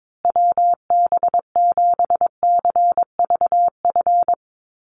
Morse code, heard as a keyed tone, about 22 wpm, 700 Hz. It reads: WB7CVF
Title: USA Call Signs Repeated 2x 22wpm